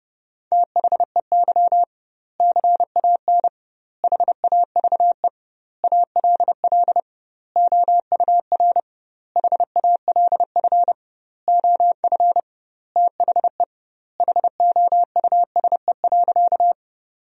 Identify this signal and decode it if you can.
Morse code, heard as a keyed tone, about 30 wpm, 700 Hz. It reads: THEY CAN HAVE ALL OUR HALF OF THE HOUSE.
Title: 2x Word-Spacing - Sentences from Top 400 Words - 30wpm